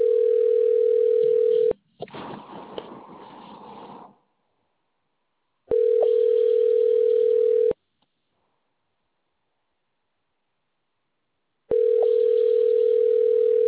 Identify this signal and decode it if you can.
Ringback tone (North America), 440 + 480 Hz.